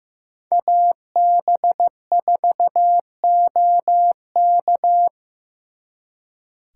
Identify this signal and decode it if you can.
Morse code, heard as a keyed tone, about 17 wpm, 700 Hz. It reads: AB4OK
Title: Speed Racing - USA 2x2 Call Signs 15wpm